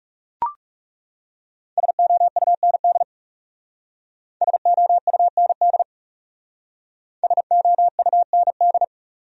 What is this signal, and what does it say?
Morse code, heard as a keyed tone, about 37 wpm, 700 Hz. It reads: SOUND SOUND SOUND
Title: Speed-Racing Top 100 Words 30wpm